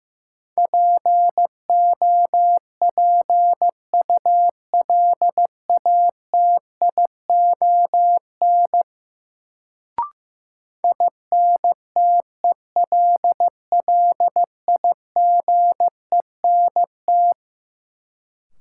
Morse code, heard as a keyed tone, about 15 wpm, 700 Hz. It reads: POPULATION INTELLIGENT
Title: Sentences from Top 1000 Words - Review Part 7 - New Word Drill - 15wpm